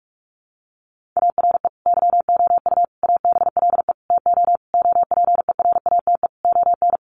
Morse code, heard as a keyed tone, about 44 wpm, 700 Hz. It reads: ARE YOU ABLE TO OPERATE ON
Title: Sentences from Top 1500 Words 45wpm